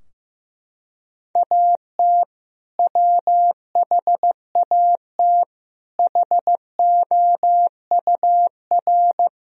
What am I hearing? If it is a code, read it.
Morse code, heard as a keyed tone, about 15 wpm, 700 Hz. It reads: AT WHAT HOUR
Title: Sentences from Top 1000 Words - Review Part 4 - 15wpm